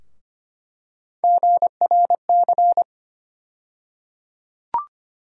Morse code, heard as a keyed tone, about 25 wpm, 700 Hz. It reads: GRC
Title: USA Call Sign Postfixes - 3 Letters 25wpm